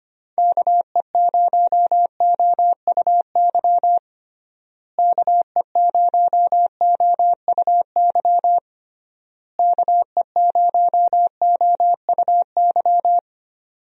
Morse code, heard as a keyed tone, about 25 wpm, 700 Hz. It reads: KE0OUY KE0OUY KE0OUY
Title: USA Call Signs - 6 Characters - Repeated 3x - 25wpm